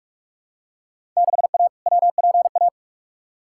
Morse code, heard as a keyed tone, about 45 wpm, 700 Hz. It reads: 6A WPA
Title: Field Day Exchanges Encore - 45wpm